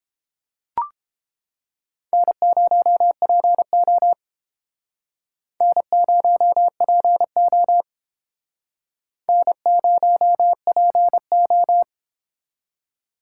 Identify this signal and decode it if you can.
Morse code, heard as a keyed tone, about 30 wpm, 700 Hz. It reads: N0PO N0PO N0PO